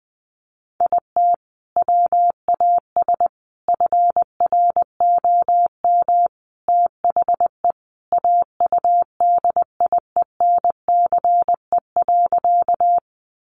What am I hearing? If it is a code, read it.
Morse code, heard as a keyed tone, about 20 wpm, 700 Hz. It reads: IT WAS FROM THE AUDIENCE.
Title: Sentences from Top 800 Words 20wpm